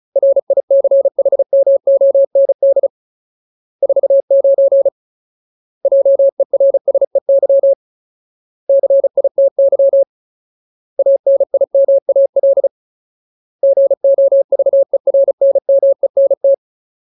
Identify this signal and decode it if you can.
Morse code, heard as a keyed tone, about 34 wpm, 550 Hz. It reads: RICHMOND 49 JERSEY CITY ANIMAL GOVERNMENT
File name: Ultimate Word List / 35WPM Ultimate